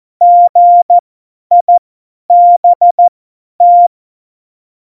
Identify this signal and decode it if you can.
Morse code, heard as a keyed tone, about 14 wpm, spaced as for 10 wpm, 700 Hz. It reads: GIBT
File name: Morsekurs / Witze / Witz 10. September 2019, 50 F70